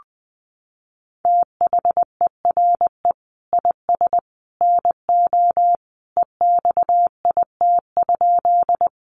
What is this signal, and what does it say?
Morse code, heard as a keyed tone, about 19 wpm, 700 Hz. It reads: THERE IS NO EXIT?
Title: Sentences from Top 1100 Words 20wpm